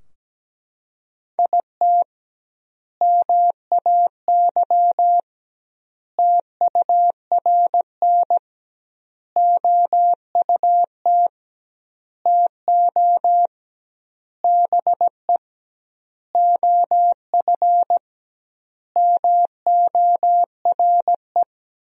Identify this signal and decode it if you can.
Morse code, heard as a keyed tone, about 17 wpm, 700 Hz. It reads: IT MAY TURN OUT TO BE OF MORE
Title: 2x Word-Spacing - Sentences from Top 400 Words - 17wpm